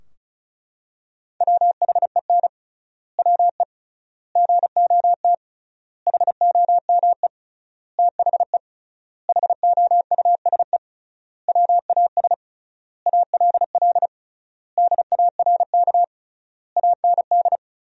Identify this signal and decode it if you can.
Morse code, heard as a keyed tone, about 34 wpm, 700 Hz. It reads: WHEN WE GOT HOME THE HOUSE WAS ALL DARK AND
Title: 3x Word-Spacing - Sentences from Top 500 Words - 35wpm